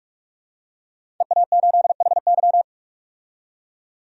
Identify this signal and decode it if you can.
Morse code, heard as a keyed tone, about 43 wpm, 700 Hz. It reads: EA8HY
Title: DXCC-100 Call Signs - 5 Characters - 45wpm